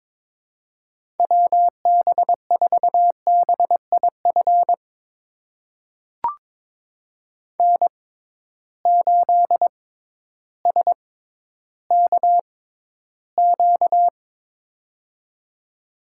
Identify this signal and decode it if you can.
Morse code, heard as a keyed tone, about 22 wpm, spaced as for 12 wpm, 700 Hz. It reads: WB4BIF N8SKQ
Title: USA Call Signs - 6x Character Spacing - 22wpm